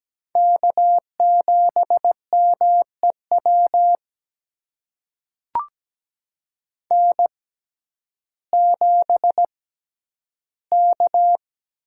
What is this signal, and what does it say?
Morse code, heard as a keyed tone, about 17 wpm, spaced as for 10 wpm, 700 Hz. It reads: K7MEW N7K
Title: USA Call Signs - 6x Character Spacing - 17wpm